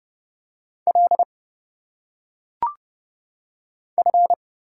Morse code, heard as a keyed tone, about 29 wpm, 700 Hz. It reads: L F
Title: Single Letters - Phonetic Pronunciation 30wpm